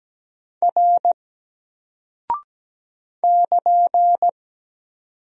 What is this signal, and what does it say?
Morse code, heard as a keyed tone, about 17 wpm, 700 Hz. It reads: R <KN>
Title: Single Character (Letter:Number:Punctuation:Prosign) 17wpm